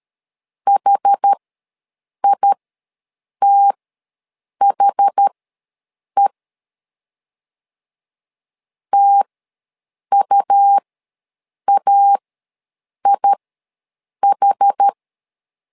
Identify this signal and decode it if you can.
Morse code, heard as a keyed tone, about 13 wpm, spaced as for 7 wpm, 800 Hz. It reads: HITHE TUAIH
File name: LEZIONI / 2 UHDN / 2 UH ws 12wpm 1730